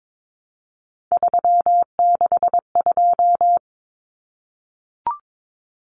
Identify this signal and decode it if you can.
Morse code, heard as a keyed tone, about 22 wpm, 700 Hz. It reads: 362
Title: Common Cut Numbers - 3-Digit 22wpm